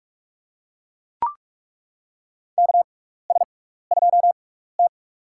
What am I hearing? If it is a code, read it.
Morse code, heard as a keyed tone, about 41 wpm, spaced as for 15 wpm, 700 Hz. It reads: KS2T
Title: USA Call Signs - 4 Characters - 6x Character Spacing - 45wpm